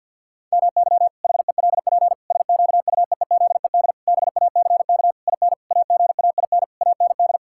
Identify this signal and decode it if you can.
Morse code, heard as a keyed tone, about 50 wpm, 700 Hz. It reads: MY HELP SQUEEZED BACK IN AGAIN AND
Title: Pangram Challenge v2 - 50wpm